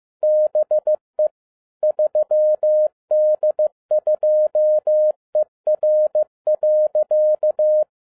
Morse code, heard as a keyed tone, about 15 wpm, 610 Hz. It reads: BE 3D2ER.
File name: 2011-10-mp3 / n07-A DOT 15wpm.mp3